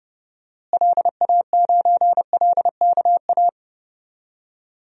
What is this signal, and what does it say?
Morse code, heard as a keyed tone, about 30 wpm, 700 Hz. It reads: LA9LKA
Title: DXCC-100 Call Signs - 6 Characters - Repeated 2x - 30wpm